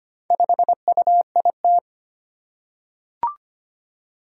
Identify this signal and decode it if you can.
Morse code, heard as a keyed tone, about 25 wpm, 700 Hz. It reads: 5UIT
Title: Four Challenging Letter-Number Combinations 25wpm